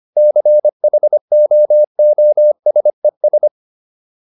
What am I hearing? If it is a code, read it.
Morse code, heard as a keyed tone, about 25 wpm, 600 Hz. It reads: CHOOSES